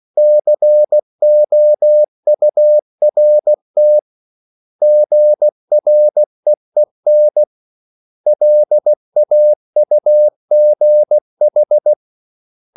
Morse code, heard as a keyed tone, about 16 wpm, 600 Hz. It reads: COURT GREEN LAUGH
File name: CW / Practice / 500MostCommonEnglishWords16